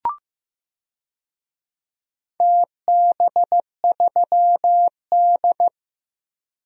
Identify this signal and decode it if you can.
Morse code, heard as a keyed tone, about 16 wpm, 700 Hz. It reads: TB3D
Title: DXCC-100 Call Signs - 4 Characters - Repeated 2x - 15wpm